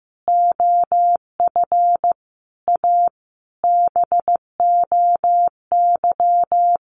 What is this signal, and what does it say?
Morse code, heard as a keyed tone, about 15 wpm, 700 Hz. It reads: OF A BOY
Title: Sentences from Top 1300 Words 15wpm